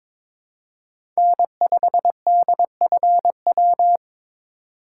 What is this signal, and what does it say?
Morse code, heard as a keyed tone, about 23 wpm, 700 Hz. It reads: N5DFW
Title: USA Call Signs - 5 Characters - 3x Character Spacing - 22wpm